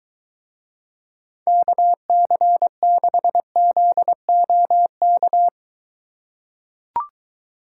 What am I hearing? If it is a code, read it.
Morse code, heard as a keyed tone, about 22 wpm, 700 Hz. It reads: KC6ZOK